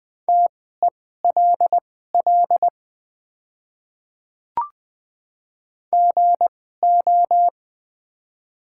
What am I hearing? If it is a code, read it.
Morse code, heard as a keyed tone, about 20 wpm, spaced as for 15 wpm, 700 Hz. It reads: TELL GO
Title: 2x Character Spacing - Top 200 Words - 20wpm